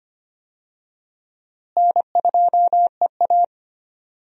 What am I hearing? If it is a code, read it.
Morse code, heard as a keyed tone, about 25 wpm, 700 Hz. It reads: N2EA
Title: USA Call Signs - 4 Characters - Repeated 3x - 25wpm